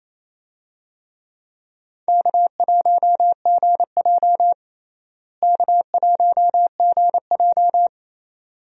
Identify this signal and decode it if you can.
Morse code, heard as a keyed tone, about 28 wpm, 700 Hz. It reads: K1GJ K1GJ
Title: USA Call Signs - 4 Characters - Repeated 2x - 28wpm